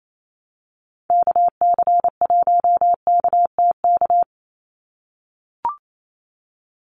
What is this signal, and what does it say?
Morse code, heard as a keyed tone, about 28 wpm, 700 Hz. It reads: KC1KTK
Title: USA Call Signs 28wpm